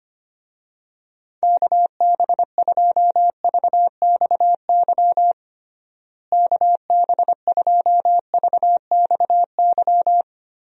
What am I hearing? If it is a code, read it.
Morse code, heard as a keyed tone, about 25 wpm, 700 Hz. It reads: KB2VXY KB2VXY
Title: USA Call Signs - 6 Characters - Repeated 3x - 25wpm